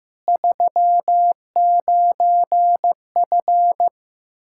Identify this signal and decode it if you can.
Morse code, heard as a keyed tone, about 15 wpm, 700 Hz. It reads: 39F
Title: Practice QSO - Exchange 2 - Info - 15wpm